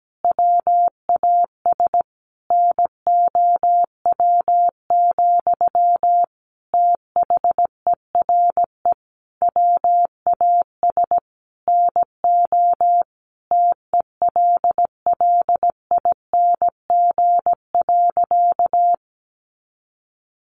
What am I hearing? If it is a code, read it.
Morse code, heard as a keyed tone, about 17 wpm, 700 Hz. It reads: WAS NOW, THERE WAS NO TELLING.
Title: A Sound Of Thunder 17wpm